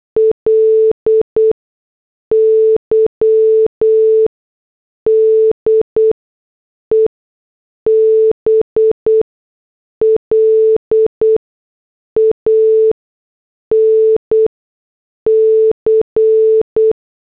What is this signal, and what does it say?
Morse code, heard as a keyed tone, about 8 wpm, spaced as for 6 wpm, 440 Hz. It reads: LY DE BLANC
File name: morse / headlines